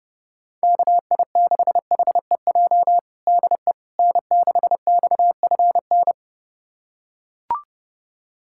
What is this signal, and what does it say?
Morse code, heard as a keyed tone, about 30 wpm, 700 Hz. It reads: KI6HEJ DE N6XFN
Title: USA Callsign DE Callsign 30wpm